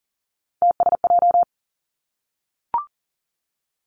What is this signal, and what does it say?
Morse code, heard as a keyed tone, about 42 wpm, 700 Hz. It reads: TSJ
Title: Three Challenging Letter-Number Combinations 40wpm